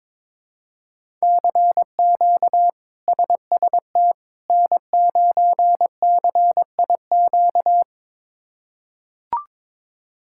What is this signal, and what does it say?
Morse code, heard as a keyed tone, about 22 wpm, 700 Hz. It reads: CQ SST N9CIQ
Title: K1USN SST QSO Practice - 22wpm